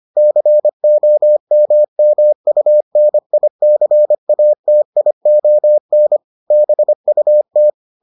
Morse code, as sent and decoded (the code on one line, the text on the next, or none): -.-. --- -- -- ..- -. .. -.-. .- - .. --- -. / -... ..- -
COMMUNICATION BUT